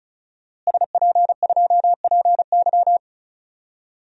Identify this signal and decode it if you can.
Morse code, heard as a keyed tone, about 34 wpm, 700 Hz. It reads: SP2PY
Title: DXCC-100 Call Signs - 5 Characters - Repeated 2x - 35wpm